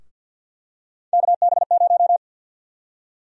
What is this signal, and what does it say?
Morse code, heard as a keyed tone, about 48 wpm, 700 Hz. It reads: KB0
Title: USA Call Sign Prefixes - Repeated 2x - 50wpm